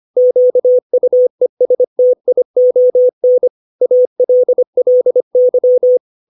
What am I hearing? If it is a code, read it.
Morse code, heard as a keyed tone, about 25 wpm, 500 Hz. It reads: QUESTION ALLY
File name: 632 words / 632 Words 25WPM